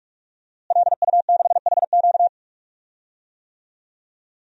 Morse code, heard as a keyed tone, about 45 wpm, 700 Hz. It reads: LU6HQ
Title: DXCC-100 Call Signs - 5 Characters - Repeated 2x - 45wpm